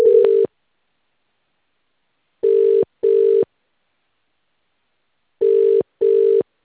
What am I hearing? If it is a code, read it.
Ringback tone (UK), 400 + 450 Hz.